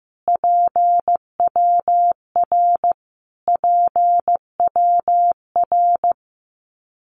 Morse code, heard as a keyed tone, about 15 wpm, 700 Hz. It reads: PWR PWR
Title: Abbreviations Repeated 3x - 15wpm